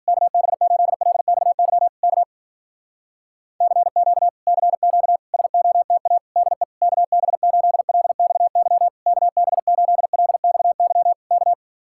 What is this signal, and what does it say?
Morse code, heard as a keyed tone, about 60 wpm, 700 Hz. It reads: KB8LXY K CQ CQ SOTA DE KB8LXY KB8LXY K